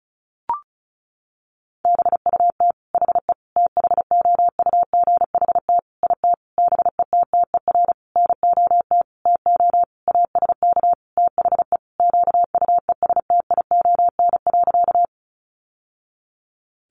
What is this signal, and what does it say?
Morse code, heard as a keyed tone, about 34 wpm, 700 Hz. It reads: BUT HE THOUGHT IT BETTER NOT TO ASK THE QUESTION.
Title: Sentences from Top 600 Words 35wpm